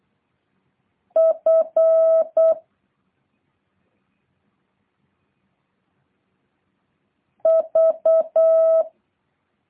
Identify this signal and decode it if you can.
Morse code, heard as a keyed tone, about 8 wpm, 650 Hz. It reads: F V